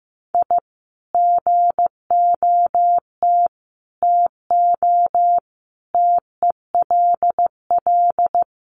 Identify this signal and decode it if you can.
Morse code, heard as a keyed tone, about 15 wpm, 700 Hz. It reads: I GOT TO TELL
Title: Sets of 5 Words Encore2 15wpm